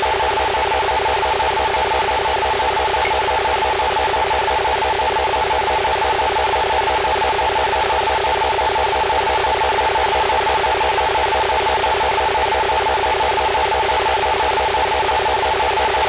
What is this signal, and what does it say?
Two-tone teleprinter signal, about 450 Hz shift.
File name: Indian Lake EMS 2023 05 03 17 56 07